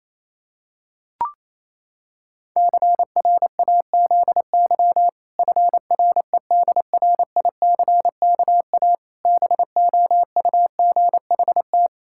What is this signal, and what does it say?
Morse code, heard as a keyed tone, about 28 wpm, 700 Hz. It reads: CRAZY FREDRICKA BOUGHT
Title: Pangram Challenge v2 - 28wpm